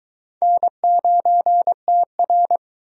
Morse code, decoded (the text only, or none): N9TR